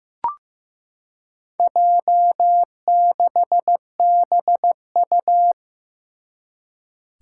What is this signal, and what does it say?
Morse code, heard as a keyed tone, about 15 wpm, 700 Hz. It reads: J6BU
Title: Four Challenging Letter-Number Combinations 15wpm